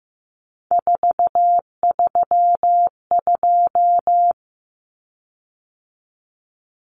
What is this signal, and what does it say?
Morse code, heard as a keyed tone, about 15 wpm, 700 Hz. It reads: 432
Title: Common Cut Numbers - 3-Digit 15wpm